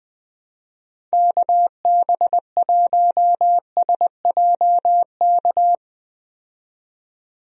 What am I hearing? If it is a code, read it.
Morse code, heard as a keyed tone, about 20 wpm, 700 Hz. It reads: KB1SJK
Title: USA Call Signs - 6 Characters - 20wpm